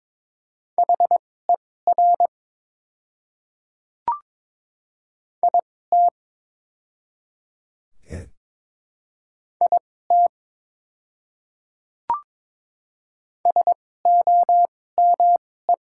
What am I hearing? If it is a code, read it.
Morse code, heard as a keyed tone, about 22 wpm, spaced as for 16 wpm, 690 Hz. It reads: HER IT IT SOME
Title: 2x Character Spacing - Top 100 Words - 22wpm